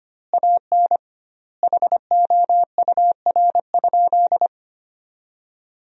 Morse code, heard as a keyed tone, about 25 wpm, 700 Hz. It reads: AN HOUR?
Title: 2x Word-Spacing - Sentences from Top 400 Words - 25wpm